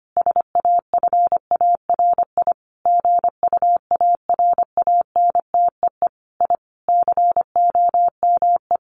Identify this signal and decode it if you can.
Morse code, heard as a keyed tone, about 25 wpm, 700 Hz. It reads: SAFARI GUARANTEE I COME
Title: A Sound Of Thunder 25wpm